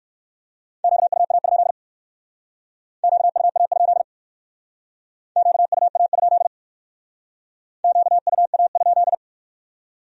Morse code, heard as a keyed tone, about 59 wpm, 700 Hz. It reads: QUA? QUA? QUA? QUA?